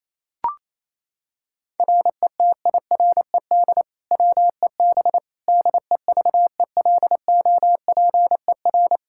Morse code, heard as a keyed tone, about 28 wpm, 700 Hz. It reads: RETIRED WEB DEVELOPER
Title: K1USN SST QSO Practice 28wpm